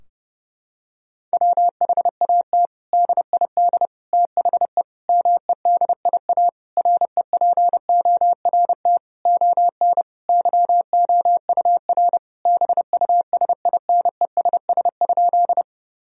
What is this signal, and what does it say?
Morse code, heard as a keyed tone, about 30 wpm, 700 Hz. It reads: WHAT DID THE MEDIA REPORT ON YOUR BUSINESS?
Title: Sentences from Top 1000 Words - Review Part 4 - 30wpm